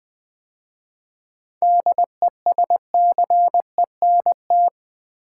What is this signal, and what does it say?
Morse code, heard as a keyed tone, about 20 wpm, 700 Hz. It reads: DESCENT
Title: 7 Letter Words - First 3 Letters Given - 20wpm